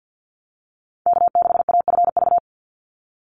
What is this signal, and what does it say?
Morse code, heard as a keyed tone, about 50 wpm, 700 Hz. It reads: K6AFV
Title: USA Call Signs 50wpm